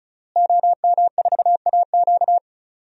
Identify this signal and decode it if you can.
Morse code, heard as a keyed tone, about 36 wpm, 700 Hz. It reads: OM4AQ
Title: DXCC-100 Call Signs - 5 Characters - Repeated 3x - 35wpm